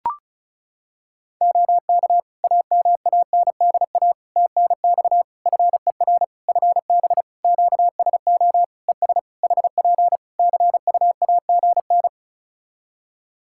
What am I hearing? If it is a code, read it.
Morse code, heard as a keyed tone, about 34 wpm, 700 Hz. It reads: OK AMANDA TNX FER FB QSO ES HP CUAGN
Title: Practice QSO - Exchange 4 - Ending - 35wpm